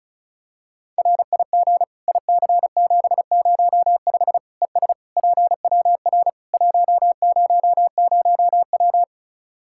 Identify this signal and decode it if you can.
Morse code, heard as a keyed tone, about 36 wpm, 700 Hz. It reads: RIG IC705 ES PWR 100W
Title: Practice QSO - Exchange 2 - Info - 35wpm